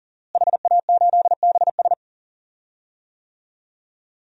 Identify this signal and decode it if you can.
Morse code, heard as a keyed tone, about 37 wpm, 700 Hz. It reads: HA8BS
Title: DXCC-100 Call Signs - 5 Characters - Repeated 2x - 40wpm